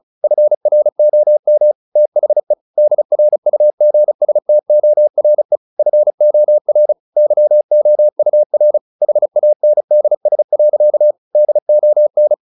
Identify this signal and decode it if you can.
Morse code, heard as a keyed tone, about 34 wpm, 600 Hz. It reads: FROM THE DRUGSTORE FOR YOUR HANDS. DON
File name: the old man and the sea 2 35wpm